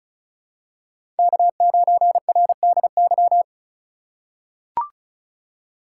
Morse code, heard as a keyed tone, about 36 wpm, 700 Hz. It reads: K9RDY
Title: USA Call Signs - 5 Characters - 2x Character Spacing - 35wpm